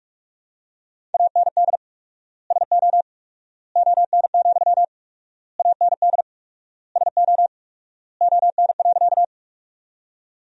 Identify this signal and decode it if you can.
Morse code, heard as a keyed tone, about 46 wpm, 700 Hz. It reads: AND SO ON, AND SO ON.
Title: 4x Word-Spacing - Sentences from Top 100 Words - 45wpm